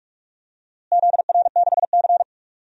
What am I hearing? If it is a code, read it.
Morse code, heard as a keyed tone, about 44 wpm, 700 Hz. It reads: ZR6C
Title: DXCC-100 Call Signs - 4 Characters - Repeated 3x - 45wpm